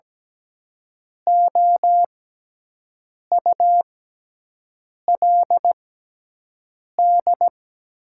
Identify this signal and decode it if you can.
Morse code, heard as a keyed tone, about 17 wpm, spaced as for 6 wpm, 700 Hz. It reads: OULD
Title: 6x Character Spacing - Top 200 Words - 17wpm